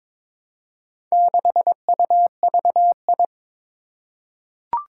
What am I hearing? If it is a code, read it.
Morse code, heard as a keyed tone, about 22 wpm, 700 Hz. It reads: 6UVI
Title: Four Challenging Letter-Number Combinations 22wpm